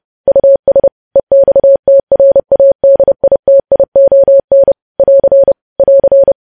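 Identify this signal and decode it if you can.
Morse code, heard as a keyed tone, about 30 wpm, 550 Hz. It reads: US EXTRADITION <AR> <AR>
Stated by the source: Morse news headlines sent at 30.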